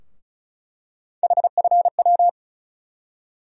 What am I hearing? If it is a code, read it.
Morse code, heard as a keyed tone, about 36 wpm, 700 Hz. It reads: HFW
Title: USA Call Sign Postfixes - 3 Letters 35wpm